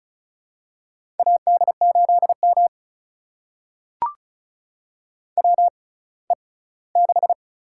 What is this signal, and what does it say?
Morse code, heard as a keyed tone, about 36 wpm, spaced as for 20 wpm, 700 Hz. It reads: AD8M WE6
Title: USA Call Signs - 4 Characters - 6x Character Spacing - 35wpm